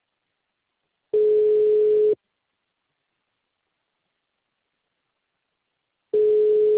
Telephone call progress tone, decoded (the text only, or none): ringback tone